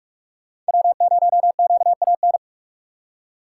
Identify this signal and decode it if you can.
Morse code, heard as a keyed tone, about 59 wpm, spaced as for 52 wpm, 700 Hz. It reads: W0QAN
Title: Speed Racing - USA 1x3 Call Signs 45wpm